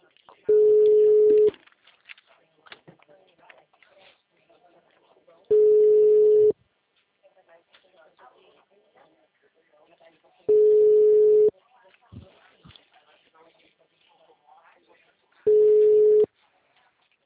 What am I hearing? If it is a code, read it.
Ringback tone (Europe, 425 Hz), with speech.